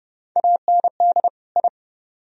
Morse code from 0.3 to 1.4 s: AND

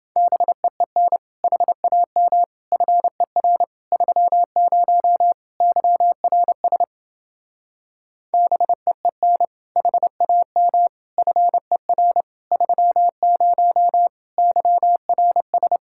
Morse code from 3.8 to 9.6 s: 30 YRS BEEN

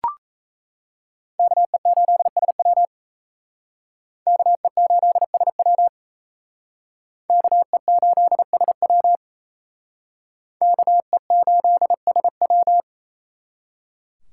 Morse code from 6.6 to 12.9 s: KE8SW KE8SW

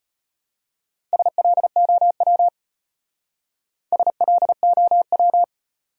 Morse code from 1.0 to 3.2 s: SLOW

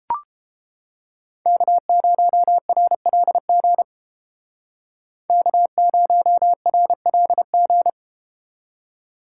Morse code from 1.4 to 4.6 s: K0RLG

